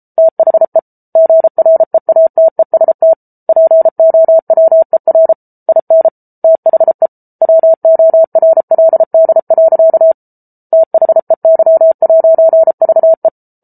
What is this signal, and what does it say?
Morse code, heard as a keyed tone, about 34 wpm, 650 Hz. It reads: THE GREATEST POWER IN THE WORLD. THEY'VE